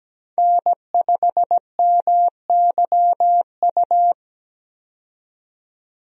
Morse code, heard as a keyed tone, about 17 wpm, 700 Hz. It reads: N5MYU
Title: USA Call Signs - 5 Characters - 17wpm